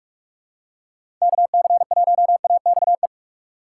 Morse code, heard as a keyed tone, about 46 wpm, 700 Hz. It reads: KC1AXE
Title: USA Call Signs - 6 Characters - 2x Character Spacing - 45wpm